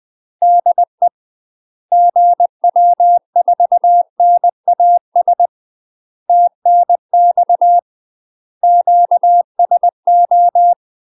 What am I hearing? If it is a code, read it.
Morse code, heard as a keyed tone, about 20 wpm, 700 Hz. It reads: DE GW4NAS TNX QSO
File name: UK MASTER QSOs TRAINING / 20WPM / UK MASTER QSOs 0030